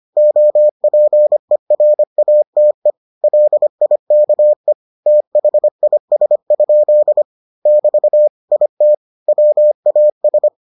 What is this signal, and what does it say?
Morse code, heard as a keyed tone, about 25 wpm, 600 Hz. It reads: OPERATE LIKE THIS? <BT> IT WAS